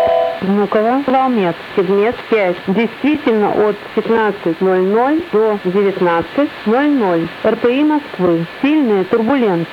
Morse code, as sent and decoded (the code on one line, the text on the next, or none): none